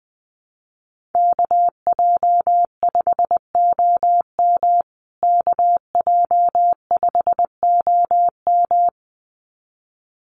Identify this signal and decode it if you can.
Morse code, heard as a keyed tone, about 20 wpm, 700 Hz. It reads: KJ5OM KJ5OM
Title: USA Call Signs Repeated 2x 20wpm